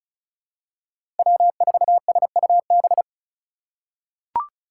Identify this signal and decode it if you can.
Morse code, heard as a keyed tone, about 36 wpm, 700 Hz. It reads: W4SUB
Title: USA Call Signs - 5 Characters - 4x Character Spacing - 35wpm